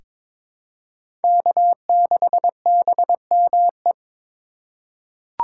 Morse code, heard as a keyed tone, about 22 wpm, 700 Hz. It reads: K6BME
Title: USA Call Signs - 8x Character Spacing - 22wpm